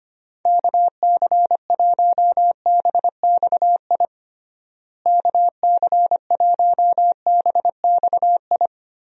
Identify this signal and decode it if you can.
Morse code, heard as a keyed tone, about 25 wpm, 700 Hz. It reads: KC1BXI KC1BXI
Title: USA Call Signs - 6 Characters - Repeated 2x - 25wpm